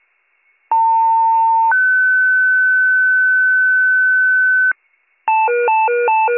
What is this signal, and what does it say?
Two-tone page, heard as a one-second tone, then a three-second tone, about 900 then 1500 Hz. It is heard from 0.7 to 4.7 s.